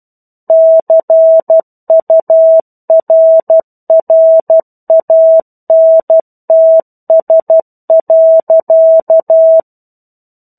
Morse code, heard as a keyed tone, about 12 wpm, 650 Hz. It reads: CURRANTS.
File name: 01-Chapter-01 12 WPM